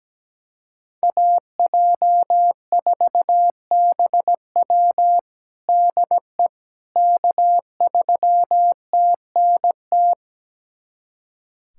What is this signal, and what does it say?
Morse code, heard as a keyed tone, about 17 wpm, 700 Hz. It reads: AJ4BW DE K3TNT
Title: USA Callsign DE Callsign - 5 Characters - 17wpm